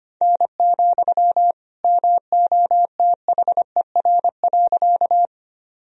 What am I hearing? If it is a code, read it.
Morse code, heard as a keyed tone, about 25 wpm, 700 Hz. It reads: N, MOTHER.
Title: Sentences from Top 200 Words and Abbreviations 25wpm